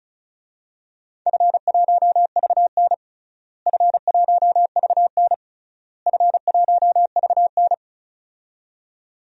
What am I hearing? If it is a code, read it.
Morse code, heard as a keyed tone, about 36 wpm, 700 Hz. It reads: F1VN F1VN F1VN
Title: DXCC-100 Call Signs - 4 Characters - Repeated 3x - 35wpm